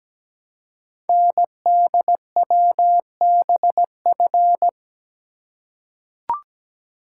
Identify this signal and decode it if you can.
Morse code, heard as a keyed tone, about 17 wpm, 700 Hz. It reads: NDWBF
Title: Code Groups - 5 Letters - 17wpm